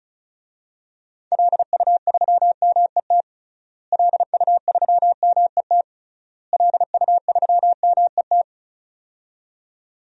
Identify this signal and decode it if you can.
Morse code, heard as a keyed tone, about 36 wpm, 700 Hz. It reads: LU3MET LU3MET LU3MET
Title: DXCC-100 Call Signs - 6 Characters - Repeated 3x - 35wpm